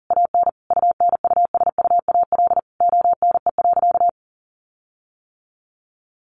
Morse code, heard as a keyed tone, about 40 wpm, 700 Hz. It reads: AN UNUSUAL ONE.
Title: Sentences from Top 1400 Words 40wpm